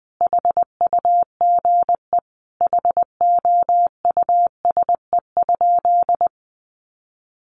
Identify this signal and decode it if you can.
Morse code, heard as a keyed tone, about 19 wpm, 700 Hz. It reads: HUGE HOUSE?
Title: Sentences from Top 1100 Words 20wpm